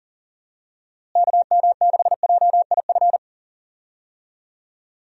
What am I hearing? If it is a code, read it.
Morse code, heard as a keyed tone, about 39 wpm, 700 Hz. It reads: KM6JIF
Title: USA Call Signs - 6 Characters - Repeated 2x - 40wpm